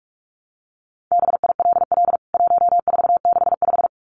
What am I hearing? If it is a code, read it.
Morse code, heard as a keyed tone, about 44 wpm, 700 Hz. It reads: BILL 1465
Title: CWT Exchange 45wpm